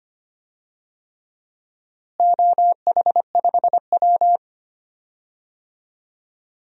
Morse code, heard as a keyed tone, about 25 wpm, 700 Hz. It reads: OH5W
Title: DXCC-100 Call Signs - 4 Characters - Repeated 3x - 25wpm